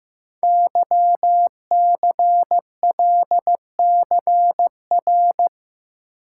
Morse code, heard as a keyed tone, about 15 wpm, 700 Hz. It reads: YCLCR
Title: Code Groups - 5 Letters - 15wpm